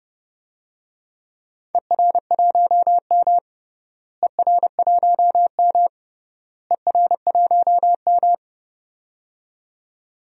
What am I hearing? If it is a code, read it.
Morse code, heard as a keyed tone, about 32 wpm, 700 Hz. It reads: ER1M ER1M ER1M
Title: DXCC-100 Call Signs - 4 Characters - Repeated 3x - 30wpm